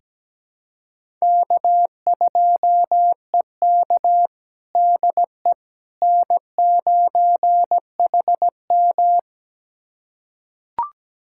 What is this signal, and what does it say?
Morse code, heard as a keyed tone, about 17 wpm, 700 Hz. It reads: K2EK DE N9HM
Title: USA Callsign DE Callsign - 4 Characters - 17wpm